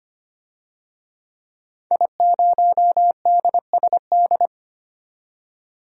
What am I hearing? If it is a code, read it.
Morse code, heard as a keyed tone, about 25 wpm, 700 Hz. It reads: I0DSD
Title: DXCC-100 Call Signs - 5 Characters - 25wpm